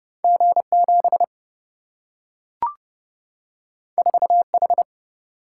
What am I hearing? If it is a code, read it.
Morse code, heard as a keyed tone, about 30 wpm, 700 Hz. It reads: G7 4H
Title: Single-Letter Single-Number 30wpm